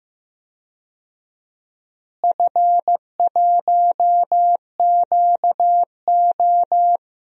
Morse code, heard as a keyed tone, about 15 wpm, 700 Hz. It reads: F1QO
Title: DXCC-100 Call Signs - 4 Characters - Repeated 2x - 15wpm